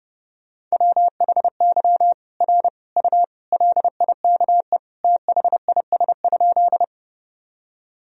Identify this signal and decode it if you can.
Morse code, heard as a keyed tone, about 30 wpm, 700 Hz. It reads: WHY R U LIKE THIS?
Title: Sentences from Top 200 Words and Abbreviations 30wpm